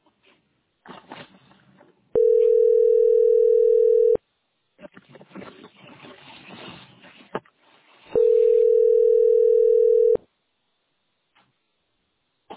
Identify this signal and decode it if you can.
Ringback tone (North America), 440 + 480 Hz.